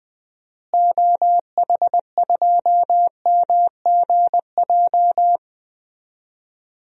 Morse code, heard as a keyed tone, about 20 wpm, 700 Hz. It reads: OH2MGJ
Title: DXCC-100 Call Signs - 6 Characters - Repeated 3x - 20wpm